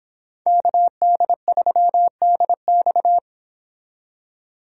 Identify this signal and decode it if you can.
Morse code, heard as a keyed tone, about 29 wpm, 700 Hz. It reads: KD3DX